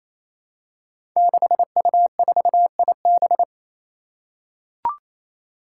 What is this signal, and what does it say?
Morse code, heard as a keyed tone, about 27 wpm, 700 Hz. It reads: 6U4IB
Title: Five Challenging Letter-Number Combinations 28wpm